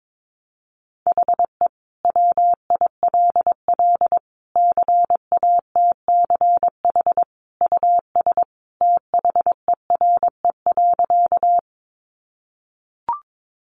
Morse code, heard as a keyed tone, about 22 wpm, 700 Hz. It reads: HE WILL CATCH US THERE.
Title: Sentences from Top 1500 Words 22wpm